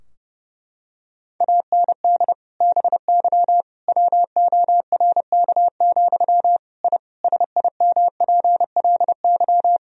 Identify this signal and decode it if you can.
Morse code, heard as a keyed tone, about 30 wpm, 700 Hz. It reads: AND BY WORK, I SIMPLY
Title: Sentences from Top 1000 Words - Review Part 4 - 30wpm